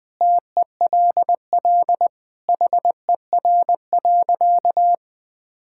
Morse code, heard as a keyed tone, about 20 wpm, 700 Hz. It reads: TELL HER.
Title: Sentences from Top 1000 Words - Review Part 5 - 20wpm